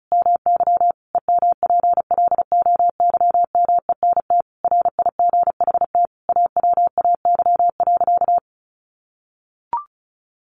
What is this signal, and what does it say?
Morse code, heard as a keyed tone, about 34 wpm, 700 Hz. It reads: MY EMPLOYMENT RIGHT AWAY.
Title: Sentences from Top 1500 Words 35wpm